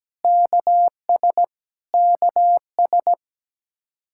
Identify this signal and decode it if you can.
Morse code, heard as a keyed tone, about 17 wpm, 700 Hz. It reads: KS KS
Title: Practice QSO - Exchange 1 - Beginning - 17wpm